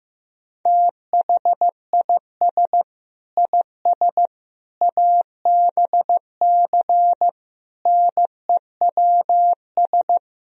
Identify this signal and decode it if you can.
Morse code, heard as a keyed tone, about 15 wpm, 700 Hz. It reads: THIS IS ABC NEWS
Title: Speed-Racing - Sets of 5 Words 15wpm